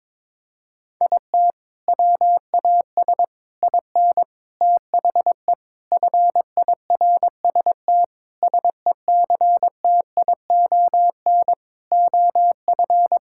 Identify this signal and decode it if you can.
Morse code, heard as a keyed tone, about 22 wpm, 700 Hz. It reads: IT WAS IN THE FIRST SECTION OF
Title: Sentences from Top 1000 Words - Review Part 5 - 22wpm